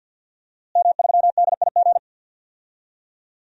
Morse code, heard as a keyed tone, about 52 wpm, 700 Hz. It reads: M3DIG
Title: DXCC-100 Call Signs - 5 Characters - Repeated 3x - 50wpm